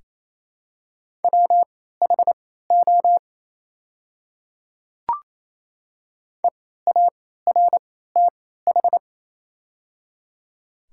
Morse code, heard as a keyed tone, about 27 wpm, spaced as for 16 wpm, 690 Hz. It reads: WHO EARTH
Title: 3x Character Spacing - Top 200 Words - 28wpm